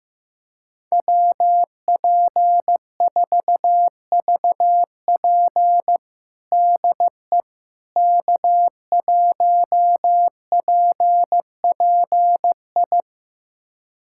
Morse code, heard as a keyed tone, about 15 wpm, 700 Hz. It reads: WP4VP DE K1PPI
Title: USA Callsign DE Callsign - 5 Characters - 15wpm